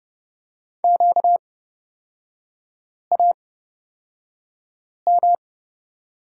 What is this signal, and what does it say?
Morse code, heard as a keyed tone, about 30 wpm, spaced as for 5 wpm, 690 Hz. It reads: QAM
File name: course / 30:10wpm / 236 Lesson 38e - Letter q - Three Characters 30wpm